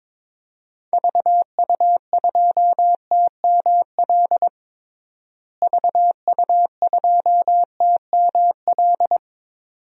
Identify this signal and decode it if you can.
Morse code, heard as a keyed tone, about 21 wpm, 700 Hz. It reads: VU2TML VU2TML
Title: DXCC-100 Call Signs - 6 Characters - Repeated 2x - 22wpm